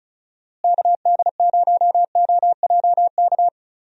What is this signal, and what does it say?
Morse code, heard as a keyed tone, about 34 wpm, 700 Hz. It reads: KD0OJK
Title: USA Call Signs - 6 Characters - Repeated 3x - 35wpm